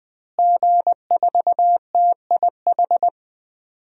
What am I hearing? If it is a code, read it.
Morse code, heard as a keyed tone, about 20 wpm, 700 Hz. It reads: G4TIH
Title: DXCC-100 Call Signs - 5 Characters - Repeated 2x - 20wpm